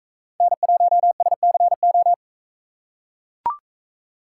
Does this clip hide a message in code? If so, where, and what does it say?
Yes: Morse code from 0.4 to 2.1 s, heard as a keyed tone, about 44 wpm, 700 Hz. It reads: N1SCO